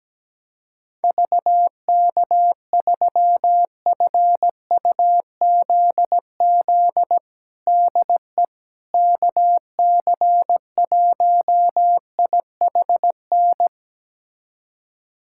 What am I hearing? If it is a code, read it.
Morse code, heard as a keyed tone, about 17 wpm, 700 Hz. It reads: VK3FUZZ DE KC1IHN
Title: USA Callsign DE Callsign 17wpm